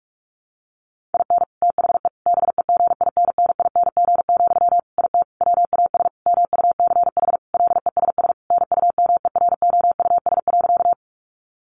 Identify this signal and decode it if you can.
Morse code, heard as a keyed tone, about 44 wpm, 700 Hz. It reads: IN THE BEGINNING, IT WAS MUCH LESS NUMEROUS.
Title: Sentences from Top 1400 Words 45wpm